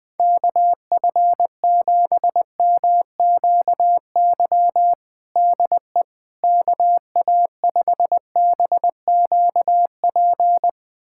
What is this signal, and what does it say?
Morse code, heard as a keyed tone, about 21 wpm, 700 Hz. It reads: KF7MQY DE KA5BQP
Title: USA Callsign DE Callsign - 6 Characters - 20wpm